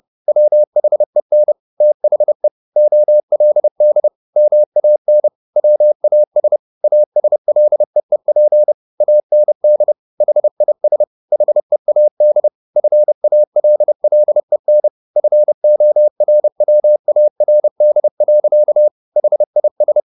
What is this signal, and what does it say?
Morse code, heard as a keyed tone, about 30 wpm, 600 Hz. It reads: WHEN THE OLD MAN WAS ASLEEP AND HIS HEAD FALLEN FORWARD. HIS
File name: the old man and the sea / the old man and the sea 1 30wpm